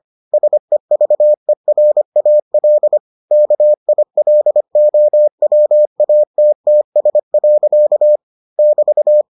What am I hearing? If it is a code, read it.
Morse code, heard as a keyed tone, about 25 wpm, 600 Hz. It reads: SEVERAL KILOWATTS. <BT>